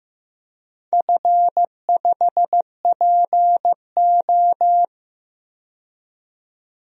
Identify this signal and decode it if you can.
Morse code, heard as a keyed tone, about 15 wpm, 700 Hz. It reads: F5PO
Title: DXCC-100 Call Signs - 4 Characters - 15wpm